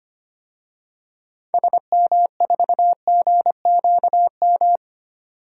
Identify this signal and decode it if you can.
Morse code, heard as a keyed tone, about 25 wpm, 700 Hz. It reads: SM4GQM
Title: DXCC-100 Call Signs - 6 Characters - 25wpm